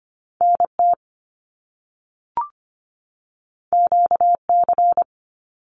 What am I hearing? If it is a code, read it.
Morse code, heard as a keyed tone, about 25 wpm, 700 Hz. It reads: NT QC
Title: Canadian Provinces and Territories 25wpm